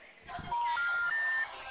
Special information tone, with noise.
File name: out- -2015- - - .3356